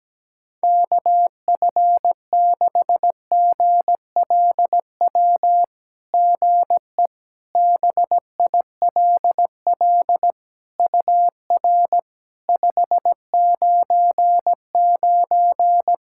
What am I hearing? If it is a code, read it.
Morse code, heard as a keyed tone, about 17 wpm, 700 Hz. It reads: KF6GLW GE BILL UR 599
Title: SOTA QSO Practice 17wpm